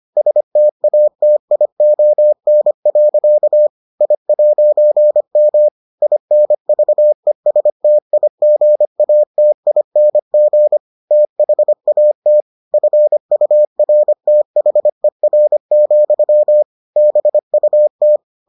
Morse code, heard as a keyed tone, about 25 wpm, 600 Hz. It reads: STATION. I'M INVESTIGATING THAT FURTHER, BUT